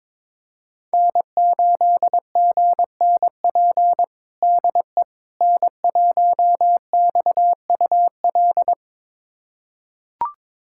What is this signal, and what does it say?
Morse code, heard as a keyed tone, about 22 wpm, 700 Hz. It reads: N8GNP DE N1XUL
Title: USA Callsign DE Callsign - 5 Characters - 22wpm